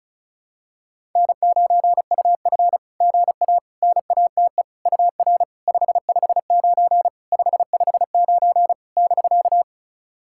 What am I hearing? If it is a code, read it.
Morse code, heard as a keyed tone, about 36 wpm, 690 Hz. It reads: N9UF GA NATE UR 559 559 <BK>
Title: SOTA QSO Practice 35wpm